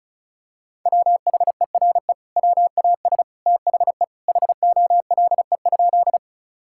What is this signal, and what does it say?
Morse code, heard as a keyed tone, about 36 wpm, 700 Hz. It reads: WHERE WAS THE HOLE?
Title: Sentences from Top 1600 Words 35wpm